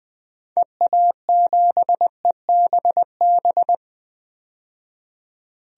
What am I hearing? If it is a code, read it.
Morse code, heard as a keyed tone, about 20 wpm, 700 Hz. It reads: EA7EBB
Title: DXCC-100 Call Signs - 6 Characters - 20wpm